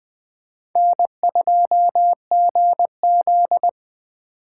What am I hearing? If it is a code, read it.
Morse code, heard as a keyed tone, about 20 wpm, 700 Hz. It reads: N2GZ
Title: USA Call Signs - 2x Character Spacing - 20wpm